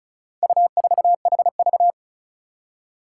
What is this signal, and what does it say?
Morse code, heard as a keyed tone, about 36 wpm, 700 Hz. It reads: U4HV
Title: Four Challenging Letter-Number Combinations 35wpm